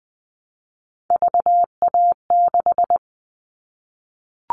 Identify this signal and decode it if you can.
Morse code, heard as a keyed tone, about 20 wpm, 700 Hz. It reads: VA6
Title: Callsign Prefixes for Top 50 DXCC - 20wpm